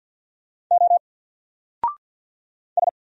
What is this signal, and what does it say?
Morse code, heard as a keyed tone, about 48 wpm, 700 Hz. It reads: O S
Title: Single Character (Letter:Number:Punctuation:Prosign) 50wpm